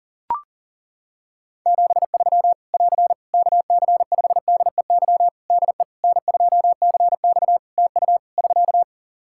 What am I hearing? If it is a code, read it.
Morse code, heard as a keyed tone, about 37 wpm, 700 Hz. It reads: 73 <AR> KC5DEY DE N2CX TU <SK>
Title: Practice QSO - Exchange 4 - Ending - 40wpm